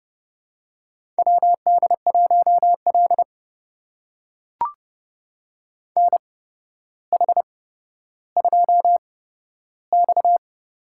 Morse code, heard as a keyed tone, about 30 wpm, spaced as for 13 wpm, 700 Hz. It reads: WD1L NH2X
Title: USA Call Signs - 4 Characters - 8x Character Spacing - 30wpm